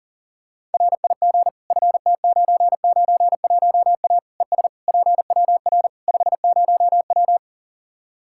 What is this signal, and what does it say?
Morse code, heard as a keyed tone, about 37 wpm, 700 Hz. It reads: RIG FT991A ES PWR 50W
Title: K1USN SST QSO Practice 40wpm